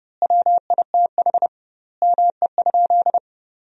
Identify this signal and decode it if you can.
Morse code, heard as a keyed tone, about 32 wpm, 700 Hz. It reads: WITH ME?
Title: 2x Word-Spacing - Sentences from Top 300 Words - 30wpm